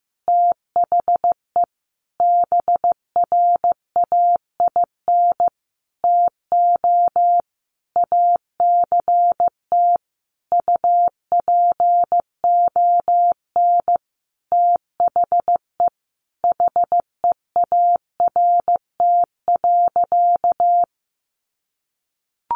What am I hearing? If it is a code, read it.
Morse code, heard as a keyed tone, about 15 wpm, 700 Hz. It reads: THE BRAIN TO ACT UPON THE HEART.
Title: Sentences from Top 1200 Words 15wpm